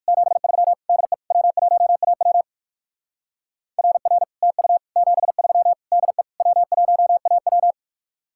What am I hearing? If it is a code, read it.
Morse code, heard as a keyed tone, about 55 wpm, 700 Hz. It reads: 73 DE W1AW RR TU 73 DE W1AW
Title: Speed-Racing Common QSO Phrases - 45wpm